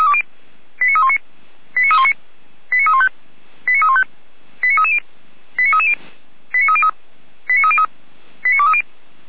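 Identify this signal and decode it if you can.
Five-tone selective call (ZVEI), heard as quick runs of five tones, tones of about 70 ms. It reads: 89319 89319 89315 89315 89330 89330 89303 89303 89230
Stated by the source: Five-tone ZVEI selective call 89125, received on SDR.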